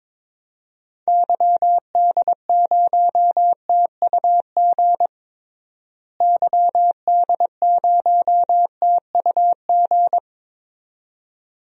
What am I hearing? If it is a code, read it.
Morse code, heard as a keyed tone, about 22 wpm, 700 Hz. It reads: YD0TUG YD0TUG
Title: DXCC-100 Call Signs - 6 Characters - Repeated 2x - 22wpm